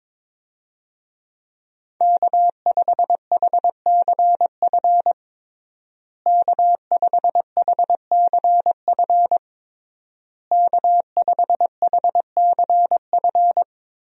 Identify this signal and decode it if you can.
Morse code, heard as a keyed tone, about 22 wpm, 690 Hz. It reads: K5HCF K5HCF K5HCF
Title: USA Call Signs - 5 Characters - Repeated 3x - 22wpm